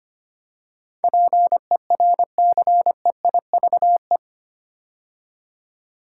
Morse code, heard as a keyed tone, about 25 wpm, 690 Hz. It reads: PERCEIVE
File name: 8 Letter Words - First 3 Letters Given - 25wpm